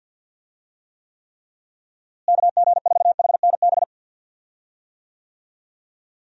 Morse code, heard as a keyed tone, about 46 wpm, 700 Hz. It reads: KG4HNB
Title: USA Call Signs - 6 Characters - Repeated 3x - 50wpm